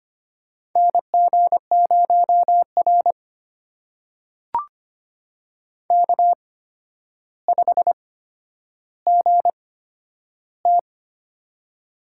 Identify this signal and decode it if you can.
Morse code, heard as a keyed tone, about 25 wpm, spaced as for 11 wpm, 700 Hz. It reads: NG0R K5GT
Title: USA Call Signs - 4 Characters - 8x Character Spacing - 25wpm